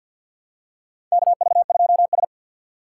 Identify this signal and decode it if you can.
Morse code, heard as a keyed tone, about 53 wpm, 700 Hz. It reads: KV2S
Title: USA Call Signs - 4x Character Spacing - 50wpm